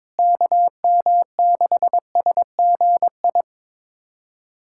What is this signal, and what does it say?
Morse code, heard as a keyed tone, about 21 wpm, 700 Hz. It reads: KM6SGI
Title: USA Callsign DE Callsign 22wpm